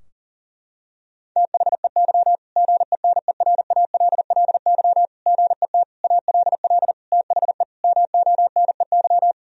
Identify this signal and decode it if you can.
Morse code, heard as a keyed tone, about 37 wpm, 700 Hz. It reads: THEY GENERALLY GET ALL THE MONEY
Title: Sentences from Top 1000 Words - Review Part 4 - 40wpm